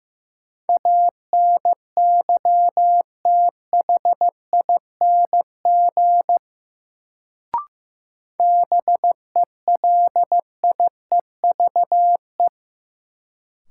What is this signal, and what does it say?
Morse code, heard as a keyed tone, about 15 wpm, 700 Hz. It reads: ANYTHING BELIEVE
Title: Sentences from Top 1000 Words - New Word Drill - 15wpm